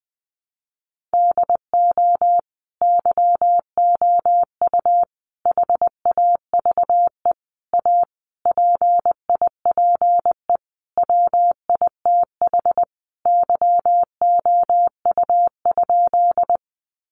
Morse code, heard as a keyed tone, about 20 wpm, 700 Hz. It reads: DO YOU HAVE A PIPE WITH YOU?
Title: Sentences from Top 1500 Words 20wpm